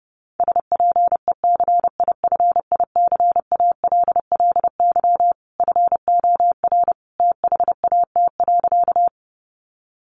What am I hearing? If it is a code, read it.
Morse code, heard as a keyed tone, about 30 wpm, 700 Hz. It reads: SPECIFICALLY FOR THAT.
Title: Sentences from Top 1100 Words 30wpm